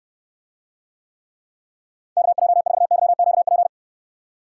Morse code, heard as a keyed tone, about 69 wpm, 700 Hz. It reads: KJ4YQP